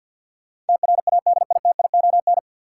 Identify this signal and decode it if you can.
Morse code, heard as a keyed tone, about 50 wpm, 700 Hz. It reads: TRADITION
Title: Sentences from Top 1500 Words - New Word Drill - 50wpm